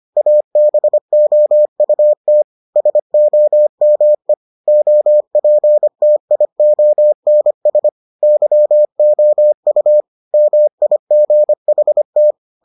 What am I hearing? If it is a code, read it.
Morse code, heard as a keyed tone, about 25 wpm, 600 Hz. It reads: ABOUT SOME OPTIONS YOU MIGHT